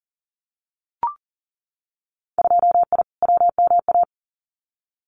Morse code, heard as a keyed tone, about 40 wpm, 700 Hz. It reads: 2I WMA
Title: Winter Field Day Exchanges Encore 40wpm